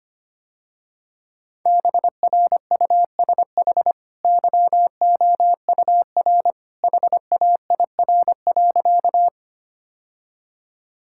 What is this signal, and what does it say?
Morse code, heard as a keyed tone, about 25 wpm, 690 Hz. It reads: BRUSH YOUR HAIR.
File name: Easy Sentences with Hints - 25wpm